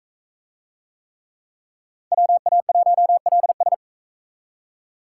Morse code, heard as a keyed tone, about 42 wpm, 700 Hz. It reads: WA1LS